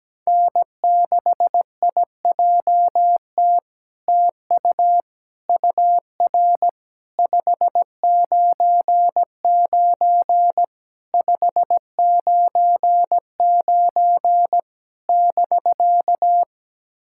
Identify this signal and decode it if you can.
Morse code, heard as a keyed tone, about 17 wpm, 700 Hz. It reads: N6IJT TU UR 599 599 <BK>
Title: SOTA QSO Practice 17wpm